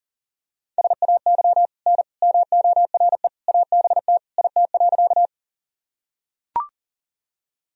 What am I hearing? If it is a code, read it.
Morse code, heard as a keyed tone, about 40 wpm, 700 Hz. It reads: SAY N MORE ABT IT.
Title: Sentences from Top 200 Words and Abbreviations 40wpm